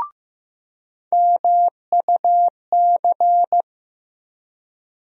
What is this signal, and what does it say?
Morse code, heard as a keyed tone, about 15 wpm, 700 Hz. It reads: MUC
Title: USA Call Sign Postfixes - 3 Letters 15wpm